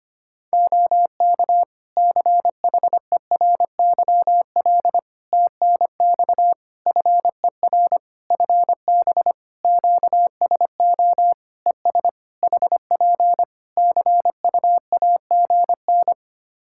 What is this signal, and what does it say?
Morse code, heard as a keyed tone, about 25 wpm, 700 Hz. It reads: OK CHERYL TNX FER FB QSO ES HP CUAGN
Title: Practice QSO - Exchange 4 - Ending - 25wpm